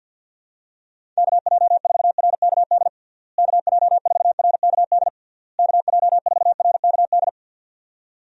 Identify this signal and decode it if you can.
Morse code, heard as a keyed tone, about 47 wpm, 700 Hz. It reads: KJ4RKD KJ4RKD KJ4RKD
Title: USA Call Signs - 6 Characters - Repeated 3x - 50wpm